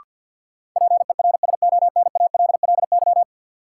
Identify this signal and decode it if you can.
Morse code, heard as a keyed tone, about 52 wpm, 700 Hz. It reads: PERSONALLY
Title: Sentences from Top 1400 Words - New Word Drill Alt - 50wpm